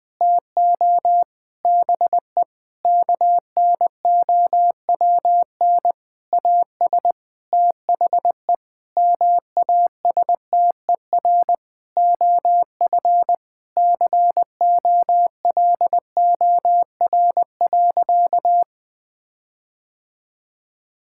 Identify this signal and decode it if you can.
Morse code, heard as a keyed tone, about 19 wpm, 700 Hz. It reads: TO BE KNOWN AS THE MASTER OF COLOR.
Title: Sentences with 6-Word Phrases - Spoken Beginning - 20wpm